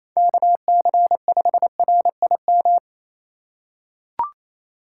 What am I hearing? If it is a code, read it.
Morse code, heard as a keyed tone, about 28 wpm, 700 Hz. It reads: KC5RIM
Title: USA Call Signs - 6 Characters - 8x Character Spacing - 28wpm